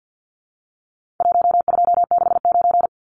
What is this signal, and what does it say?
Morse code, heard as a keyed tone, about 50 wpm, 700 Hz. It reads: 1269
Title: Numbers - Contest Style - 4-Digit 50wpm